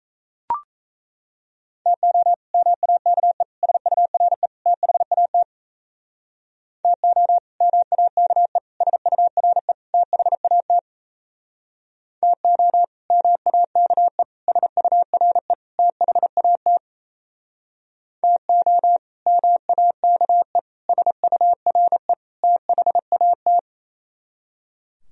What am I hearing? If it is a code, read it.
Morse code, heard as a keyed tone, about 34 wpm, 700 Hz. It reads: TO MAKE SURE THAT TO MAKE SURE THAT TO MAKE SURE THAT TO MAKE SURE THAT
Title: Speed-Racing - Sets of 4 Words 28wpm